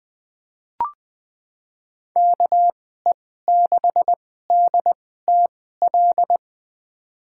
Morse code, heard as a keyed tone, about 21 wpm, spaced as for 15 wpm, 700 Hz. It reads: KE6DTL
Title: USA Call Signs - 2x Character Spacing - 20wpm